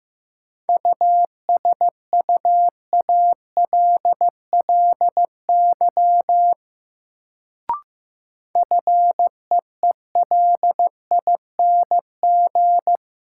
Morse code, heard as a keyed tone, about 15 wpm, 700 Hz. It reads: USUALLY FEELING
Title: Sentences from Top 1000 Words - Review Part 3 - New Word Drill - 15wpm